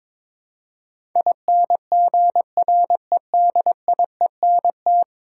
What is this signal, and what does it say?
Morse code, heard as a keyed tone, about 22 wpm, 700 Hz. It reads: INGREDIENT
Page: morse-code-ninja-practice-22wpm